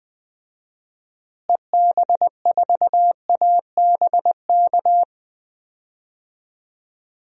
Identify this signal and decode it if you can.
Morse code, heard as a keyed tone, about 19 wpm, 700 Hz. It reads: EB4ABK
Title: DXCC-100 Call Signs - 6 Characters - Repeated 3x - 20wpm